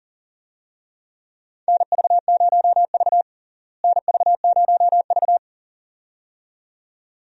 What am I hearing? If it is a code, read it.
Morse code, heard as a keyed tone, about 40 wpm, 700 Hz. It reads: NV0V NV0V
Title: USA Call Signs - 4 Characters - Repeated 2x - 40wpm